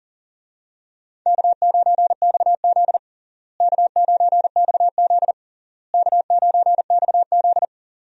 Morse code, heard as a keyed tone, about 40 wpm, 700 Hz. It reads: K9XZ K9XZ K9XZ
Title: USA Call Signs - 4 Characters - Repeated 3x - 40wpm